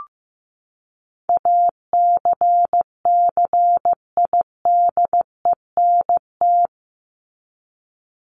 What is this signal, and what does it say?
Morse code, heard as a keyed tone, about 15 wpm, 700 Hz. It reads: ACCIDENT
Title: Top 1751-2000 Words 15wpm